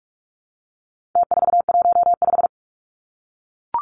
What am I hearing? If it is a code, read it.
Morse code, heard as a keyed tone, about 44 wpm, 700 Hz. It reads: T415
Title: Numbers - Contest Style - 4-Digit 45wpm